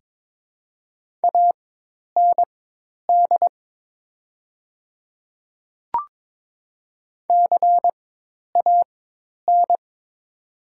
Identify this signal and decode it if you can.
Morse code, heard as a keyed tone, about 22 wpm, spaced as for 10 wpm, 700 Hz. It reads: AND CAN
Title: 4x Character Spacing - Top 300 Words - 22wpm